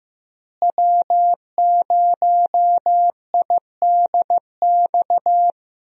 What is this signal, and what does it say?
Morse code, heard as a keyed tone, about 15 wpm, 700 Hz. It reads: W0IDX
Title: USA Call Signs - 5 Characters - Repeated 3x - 15wpm